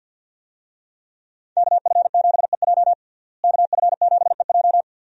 Morse code, heard as a keyed tone, about 52 wpm, 700 Hz. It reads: KF7EJ KF7EJ
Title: USA Call Signs - 5 Characters - Repeated 3x - 50wpm